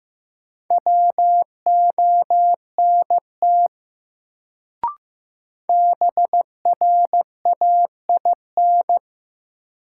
Morse code, heard as a keyed tone, about 15 wpm, 700 Hz. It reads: WONT BRAIN
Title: Sentences from Top 1000 Words - Review Part 6 - New Word Drill - 15wpm